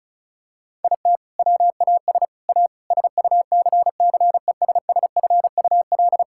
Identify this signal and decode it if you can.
Morse code, heard as a keyed tone, about 36 wpm, 700 Hz. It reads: IT WAS A SUCCESSFUL
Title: Sentences from Top 1000 Words - Review Part 6 - 35wpm